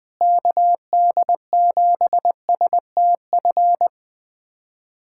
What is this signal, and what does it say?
Morse code, heard as a keyed tone, about 20 wpm, 700 Hz. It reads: KD7STF
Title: USA Callsign DE Callsign 20wpm